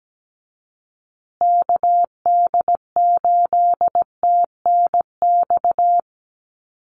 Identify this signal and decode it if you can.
Morse code, heard as a keyed tone, about 17 wpm, 700 Hz. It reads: KD8TNX
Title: USA Call Signs Repeated 3x 17wpm